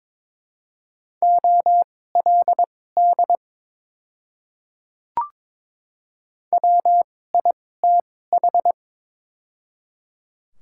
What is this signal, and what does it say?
Morse code, heard as a keyed tone, about 21 wpm, spaced as for 16 wpm, 700 Hz. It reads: OLD WITH
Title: 2x Character Spacing - Top 200 Words - 22wpm